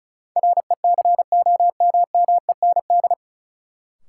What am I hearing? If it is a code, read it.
Morse code, heard as a keyed tone, about 34 wpm, 700 Hz. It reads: RECOMMEND
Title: Sentences from Top 1200 Words - New Word Drill - 35wpm